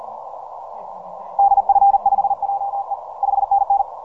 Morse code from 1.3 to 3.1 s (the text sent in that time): DF5JE